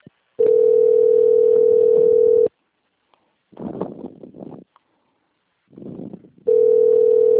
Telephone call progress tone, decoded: ringback tone

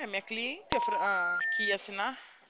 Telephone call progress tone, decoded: special information tone

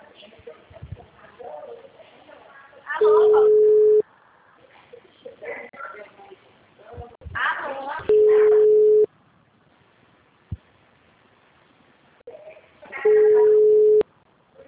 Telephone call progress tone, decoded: ringback tone

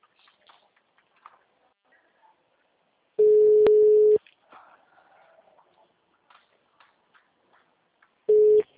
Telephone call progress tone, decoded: ringback tone

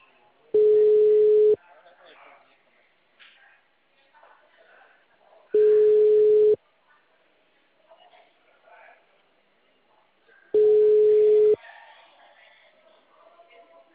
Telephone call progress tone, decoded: ringback tone

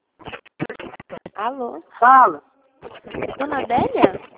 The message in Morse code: none

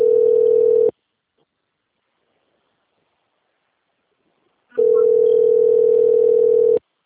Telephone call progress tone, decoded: ringback tone

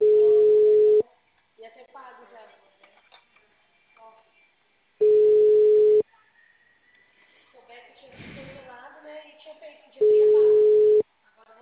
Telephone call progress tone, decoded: ringback tone

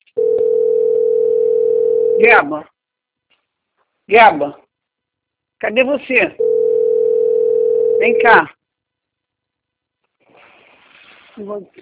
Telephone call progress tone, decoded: ringback tone

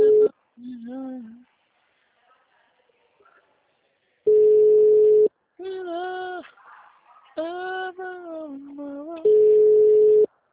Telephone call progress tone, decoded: ringback tone

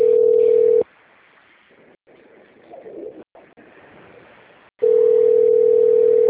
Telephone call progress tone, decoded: ringback tone

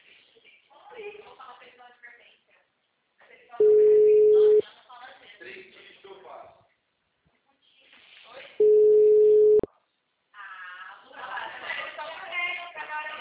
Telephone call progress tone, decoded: ringback tone

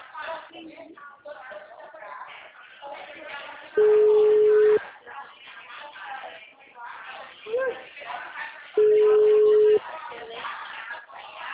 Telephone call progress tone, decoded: ringback tone

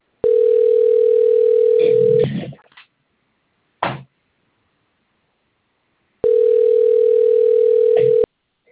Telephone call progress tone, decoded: ringback tone

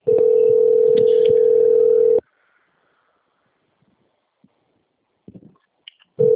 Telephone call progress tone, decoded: ringback tone

